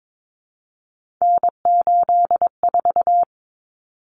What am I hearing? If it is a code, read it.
Morse code, heard as a keyed tone, about 22 wpm, 700 Hz. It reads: N84
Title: Common Cut Numbers - 3-Digit 22wpm